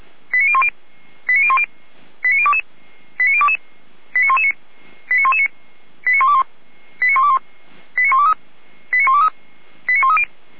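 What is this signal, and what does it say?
Five-tone selective call (ZVEI), heard as quick runs of five tones, tones of about 70 ms. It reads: 89010 89010 89022 89022 89119 89119 89121 89121 89123 89123 89130
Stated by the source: ZVEI five-tone selective call 89011, received on an SDR.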